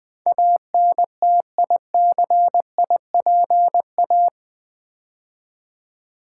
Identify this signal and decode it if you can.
Morse code, heard as a keyed tone, about 20 wpm, 700 Hz. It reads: ANTICIPA
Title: Build A Word - 4 Letters At A Time - 10-Letter Words - 20wpm